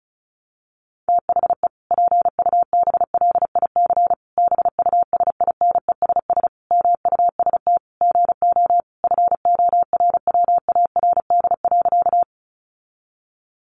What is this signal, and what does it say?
Morse code, heard as a keyed tone, about 34 wpm, 700 Hz. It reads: THE PUBLIC BUSINESS MUST GO FORWARD.
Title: Sentences from Top 1300 Words 35wpm